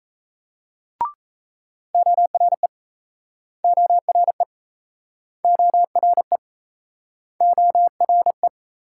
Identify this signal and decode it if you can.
Morse code, heard as a keyed tone, about 37 wpm, 700 Hz. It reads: ORE ORE ORE ORE